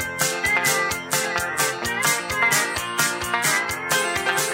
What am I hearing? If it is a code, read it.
Emergency Alert System (SAME) tones, heard as harsh buzzing bursts. It is heard from 0.3 to 4.0 s.